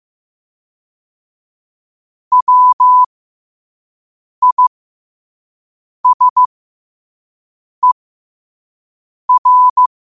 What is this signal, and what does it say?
Morse code, heard as a keyed tone, about 15 wpm, spaced as for 5 wpm, 1000 Hz. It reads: WISER
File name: qotdfiles / 5wpm